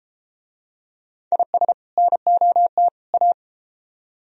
Morse code, heard as a keyed tone, about 31 wpm, 700 Hz. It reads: IS NOT A